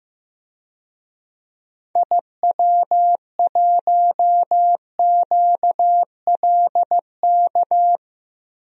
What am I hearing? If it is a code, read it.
Morse code, heard as a keyed tone, about 15 wpm, 700 Hz. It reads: IW1QLK
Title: DXCC-100 Call Signs - 6 Characters - Repeated 2x - 15wpm